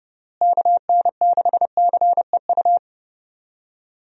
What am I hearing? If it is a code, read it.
Morse code, heard as a keyed tone, about 32 wpm, 700 Hz. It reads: KN6CEU
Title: USA Call Signs - 6 Characters - Repeated 2x - 30wpm